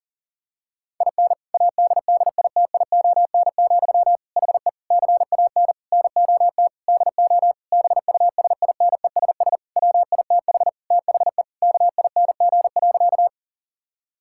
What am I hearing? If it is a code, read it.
Morse code, heard as a keyed tone, about 40 wpm, 700 Hz. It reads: IN ADDITION, HE CAN NOT DO BUSINESS WITH THE KING.
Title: Sentences from Top 1000 Words - Review Part 6 - 40wpm